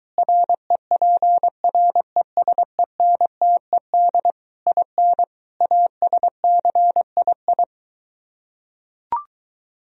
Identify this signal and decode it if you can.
Morse code, heard as a keyed tone, about 23 wpm, 700 Hz. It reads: REPRESENTED IN ASCII